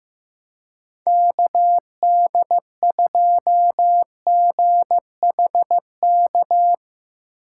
Morse code, heard as a keyed tone, about 15 wpm, 700 Hz. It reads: KD2GHK
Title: USA Call Signs - 3x Character Spacing - 15wpm